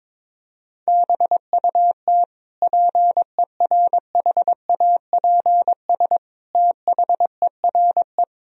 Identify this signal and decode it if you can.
Morse code, heard as a keyed tone, about 22 wpm, 700 Hz. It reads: BUT PERHAPS THERE
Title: Sentences from Top 1000 Words - Review Part 3 - 22wpm